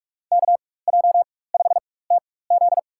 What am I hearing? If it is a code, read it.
Morse code, heard as a keyed tone, about 46 wpm, spaced as for 21 wpm, 700 Hz. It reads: KJ5TZ
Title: USA Call Signs - 5 Characters - 4x Character Spacing - 45wpm